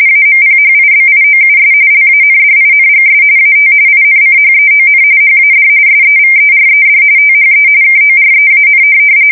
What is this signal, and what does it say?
RTTY, about 45.45 baud. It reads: AMPHIBIOUS LANDINGS AT INCHEON HAD SEVERED THE KOREAN
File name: fsk170 / USS-Missouri